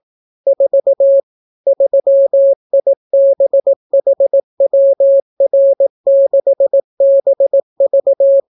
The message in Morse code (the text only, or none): 4 3IBHWR6BV